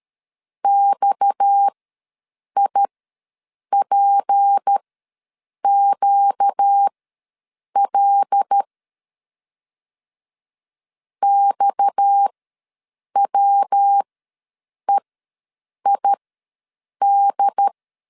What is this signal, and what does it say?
Morse code, heard as a keyed tone, about 13 wpm, spaced as for 7 wpm, 800 Hz. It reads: XIPQL XWEID
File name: LEZIONI / 6 PKX0 / 6 PKX ws 12wpm 3106